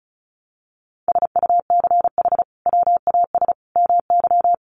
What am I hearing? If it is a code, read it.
Morse code, heard as a keyed tone, about 34 wpm, 700 Hz. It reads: SUCH WAS MY
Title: Sentences from Top 800 Words 35wpm